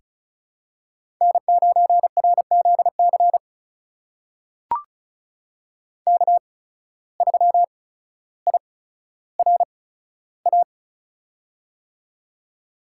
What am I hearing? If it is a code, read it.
Morse code, heard as a keyed tone, about 36 wpm, spaced as for 15 wpm, 700 Hz. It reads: N9RZC K3IRA
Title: USA Call Signs - 5 Characters - 8x Character Spacing - 35wpm